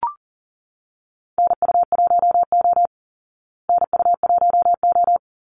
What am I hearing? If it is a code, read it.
Morse code, heard as a keyed tone, about 40 wpm, 700 Hz. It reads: NU1O NU1O
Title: Call Signs Repeated 2x - 40wpm